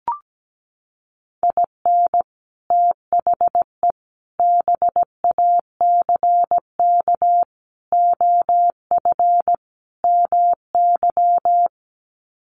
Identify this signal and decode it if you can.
Morse code, heard as a keyed tone, about 17 wpm, 700 Hz. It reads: IN THE BACK OF MY
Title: Sets of 5 Words Encore 17wpm